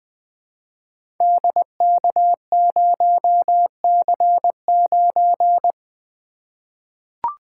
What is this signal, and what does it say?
Morse code, heard as a keyed tone, about 20 wpm, 700 Hz. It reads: DK0C9
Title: Code Groups - 5 Letter:Numbers - 20wpm